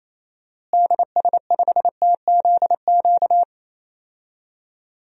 Morse code, heard as a keyed tone, about 28 wpm, 700 Hz. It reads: DS5TZQ
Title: DXCC-100 Call Signs - 6 Characters - Repeated 3x - 28wpm